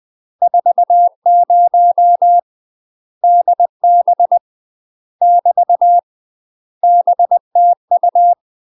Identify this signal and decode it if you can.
Morse code, heard as a keyed tone, about 20 wpm, 700 Hz. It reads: 40 DB <BT> BTU
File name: UK MASTER QSOs TRAINING / 20WPM / UK MASTER QSOs 0029